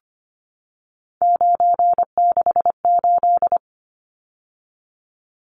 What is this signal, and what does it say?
Morse code, heard as a keyed tone, about 25 wpm, 700 Hz. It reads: 968
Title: Numbers - 3-Digit 25wpm